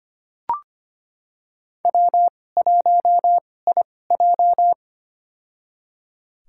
Morse code, heard as a keyed tone, about 25 wpm, spaced as for 18 wpm, 700 Hz. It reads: W1IJ
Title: USA Call Signs - 2x Character Spacing - 25wpm